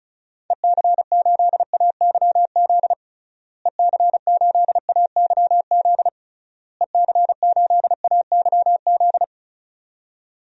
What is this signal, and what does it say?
Morse code, heard as a keyed tone, about 34 wpm, 700 Hz. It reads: EC8AYZ EC8AYZ EC8AYZ
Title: DXCC-100 Call Signs - 6 Characters - Repeated 3x - 35wpm